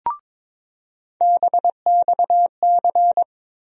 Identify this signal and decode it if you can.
Morse code, heard as a keyed tone, about 22 wpm, 700 Hz. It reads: BXC
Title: USA Call Sign Postfixes - 3 Letters 22wpm